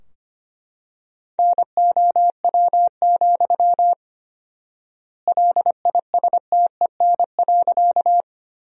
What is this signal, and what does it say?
Morse code, heard as a keyed tone, about 25 wpm, 700 Hz. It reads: NOW, LISTEN.
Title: 4x Word-Spacing - Sentences from Top 400 Words - 25wpm